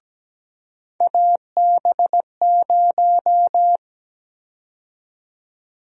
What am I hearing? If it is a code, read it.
Morse code, heard as a keyed tone, about 17 wpm, 700 Hz. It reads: AB0
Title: USA Call Sign Prefixes - 17wpm